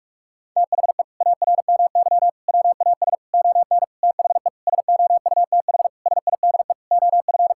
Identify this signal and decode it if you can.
Morse code, heard as a keyed tone, about 41 wpm, 700 Hz. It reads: THE ARMY WAS ON THE SOUTH SIDE OF
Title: Sentences from Top 1000 Words - Review Part 5 - 45wpm